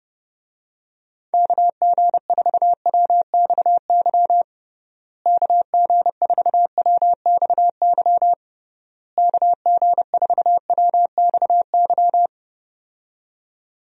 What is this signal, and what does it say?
Morse code, heard as a keyed tone, about 30 wpm, 700 Hz. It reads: KG4WXY KG4WXY KG4WXY
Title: USA Call Signs - 6 Characters - Repeated 3x - 30wpm